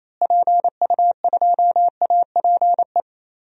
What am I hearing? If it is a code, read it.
Morse code, heard as a keyed tone, about 28 wpm, 700 Hz. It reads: PU2APE
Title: DXCC-100 Call Signs - 6 Characters - Repeated 3x - 28wpm